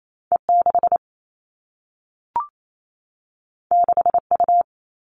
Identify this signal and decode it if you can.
Morse code, heard as a keyed tone, about 28 wpm, 700 Hz. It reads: E6 6U
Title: All Cut Numbers - 2-Digit 28wpm